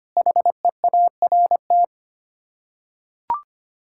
Morse code, heard as a keyed tone, about 25 wpm, 700 Hz. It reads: HEART
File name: Sets of 9 Words - 25wpm